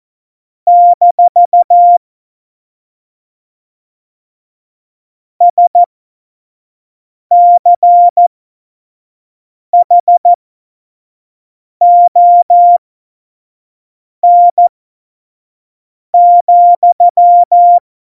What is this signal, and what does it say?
Morse code, heard as a keyed tone, about 14 wpm, spaced as for 5 wpm, 700 Hz. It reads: - SCHON,
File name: 25 F70